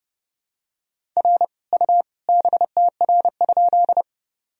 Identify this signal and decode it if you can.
Morse code, heard as a keyed tone, about 31 wpm, 700 Hz. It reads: R U BTR?
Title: Sentences from Top 400 Words and Abbreviations 30wpm